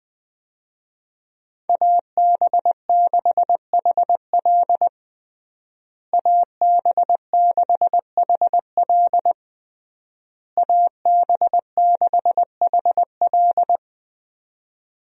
Morse code, heard as a keyed tone, about 20 wpm, 700 Hz. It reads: AB6HL AB6HL AB6HL
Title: USA Call Signs - 5 Characters - Repeated 3x - 20wpm